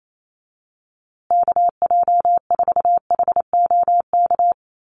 Morse code, heard as a keyed tone, about 28 wpm, 700 Hz. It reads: KJ4HOK
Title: USA Call Signs Repeated 2x 28wpm